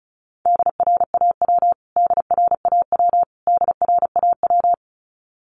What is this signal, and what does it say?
Morse code, heard as a keyed tone, about 34 wpm, 700 Hz. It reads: DRAW DRAW DRAW
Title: Top 500 Words Repeated 35wpm